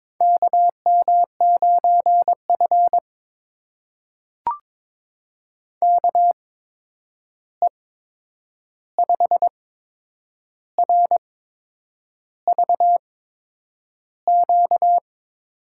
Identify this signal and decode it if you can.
Morse code, heard as a keyed tone, about 21 wpm, spaced as for 8 wpm, 700 Hz. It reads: KM9F KE5RVQ
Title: USA Call Signs - 8x Character Spacing - 22wpm